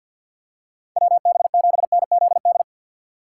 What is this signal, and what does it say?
Morse code, heard as a keyed tone, about 52 wpm, 700 Hz. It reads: WB7NZD
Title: USA Call Signs - 6 Characters - 6x Character Spacing - 50wpm